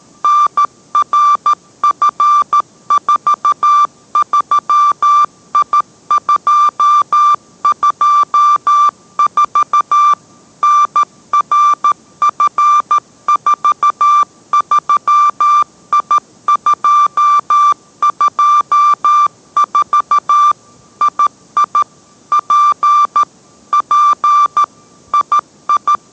Morse code, heard as a keyed tone, about 13 wpm, 1180 Hz. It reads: NRF43I224 NRF43I224 II P P II